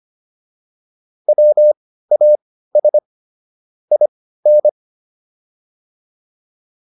Morse code, heard as a keyed tone, about 25 wpm, spaced as for 15 wpm, 600 Hz. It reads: WAS IN